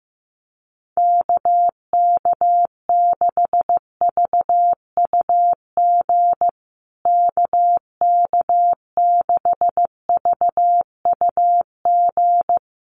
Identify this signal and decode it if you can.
Morse code, heard as a keyed tone, about 15 wpm, 700 Hz. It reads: KK6VUG KK6VUG
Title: USA Call Signs Repeated 3x 15wpm